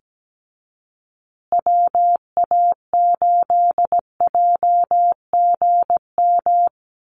Morse code, heard as a keyed tone, about 17 wpm, 700 Hz. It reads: WA8JGM
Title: USA Call Signs Repeated 2x 17wpm